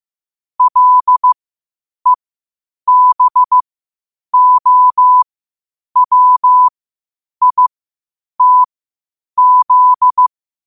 Morse code, heard as a keyed tone, about 15 wpm, spaced as for 8 wpm, 1000 Hz. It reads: LEBOWITZ
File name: qotdfiles / 8wpm